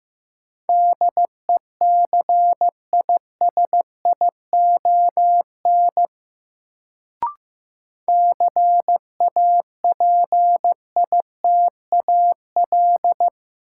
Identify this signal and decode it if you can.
Morse code, heard as a keyed tone, about 15 wpm, 700 Hz. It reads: DECISION CAPITAL
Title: Sentences from Top 1000 Words - New Word Drill - 15wpm